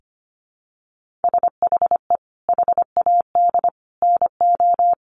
Morse code, heard as a keyed tone, about 25 wpm, 700 Hz. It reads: SHE HAD NO
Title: Sentences from Top 1500 Words 25wpm